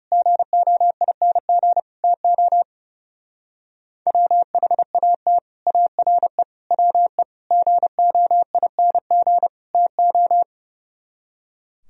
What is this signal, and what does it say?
Morse code, heard as a keyed tone, about 37 wpm, 700 Hz. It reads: GOING TO WHAT ARE WE GOING TO
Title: Speed-Racing - Sets of 5 Words 30wpm